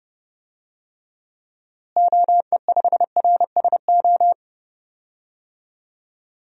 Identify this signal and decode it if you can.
Morse code, heard as a keyed tone, about 30 wpm, 700 Hz. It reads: OE5RSO
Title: DXCC-100 Call Signs - 6 Characters - Repeated 2x - 30wpm